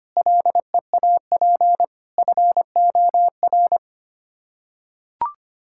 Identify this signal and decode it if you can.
Morse code, heard as a keyed tone, about 25 wpm, 700 Hz. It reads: LEAP FOR
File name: Sets of 9 Words - 25wpm